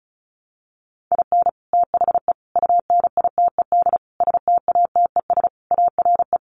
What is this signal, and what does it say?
Morse code, heard as a keyed tone, about 34 wpm, 700 Hz. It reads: IN THE UNITED STATES ARE
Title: Sets of 5 Words Encore 35wpm